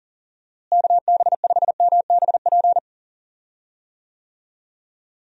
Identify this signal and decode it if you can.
Morse code, heard as a keyed tone, about 37 wpm, 700 Hz. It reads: KB5MBP
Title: USA Call Signs - 6 Characters - Repeated 2x - 40wpm